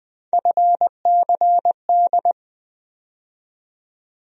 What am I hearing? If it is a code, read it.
Morse code, heard as a keyed tone, about 21 wpm, 700 Hz. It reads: FCD
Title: USA Call Sign Postfixes - 3 Letters 20wpm